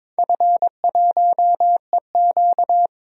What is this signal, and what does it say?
Morse code, heard as a keyed tone, about 22 wpm, 700 Hz. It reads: F1EQ
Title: DXCC-100 Call Signs - 4 Characters - 22wpm